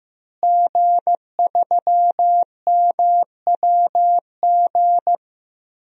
Morse code, heard as a keyed tone, about 16 wpm, 700 Hz. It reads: G3MWG
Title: DXCC-100 Call Signs - 5 Characters - Repeated 3x - 15wpm